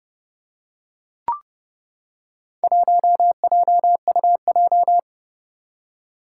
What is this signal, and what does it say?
Morse code, heard as a keyed tone, about 29 wpm, 700 Hz. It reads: 1JUJ
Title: Four Challenging Letter-Number Combinations 30wpm